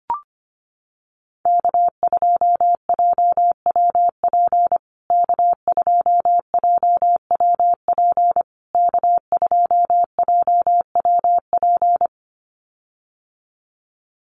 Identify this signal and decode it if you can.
Morse code, heard as a keyed tone, about 25 wpm, 700 Hz. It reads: K2JWP K2JWP K2JWP
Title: USA Call Signs Repeated 3x 25wpm